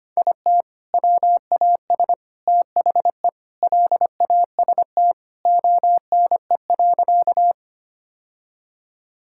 Morse code, heard as a keyed tone, about 25 wpm, 700 Hz. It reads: IT WAS THE LAST ONE.
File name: practice / 25wpm / Sentences with 3-Word Phrases - 25wpm